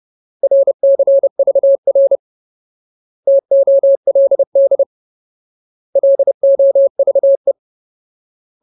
Morse code, heard as a keyed tone, about 30 wpm, 550 Hz. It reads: RCVR TOLD LOVE
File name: Ultimate Word List / 30WPM Ultimate